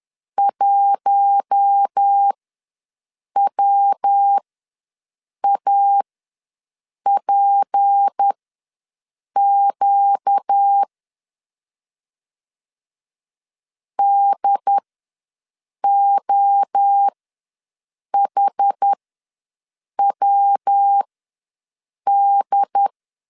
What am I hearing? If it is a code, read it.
Morse code, heard as a keyed tone, about 10 wpm, spaced as for 6 wpm, 800 Hz. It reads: 1WAPQ DOHWD
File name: LEZIONI / 6 PKX0 / 6 P ws 10wpm 3751